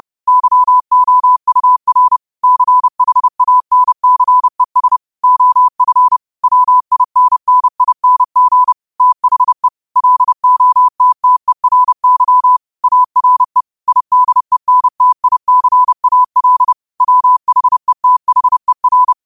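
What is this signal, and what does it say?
Morse code, heard as a keyed tone, about 30 wpm, 1000 Hz. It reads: YOUR CHANCES OF WINNING THE LOTTERY ARE IDENTICAL WHETHER
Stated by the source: Morse code quote of the day sent at 30 wpm.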